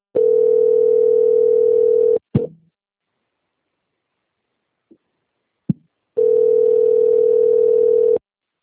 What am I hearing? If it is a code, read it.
Ringback tone (North America), 440 + 480 Hz.